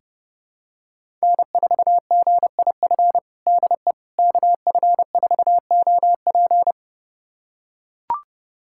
Morse code, heard as a keyed tone, about 30 wpm, 700 Hz. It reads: N4GIF DE KF4OP
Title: USA Callsign DE Callsign - 5 Characters - 30wpm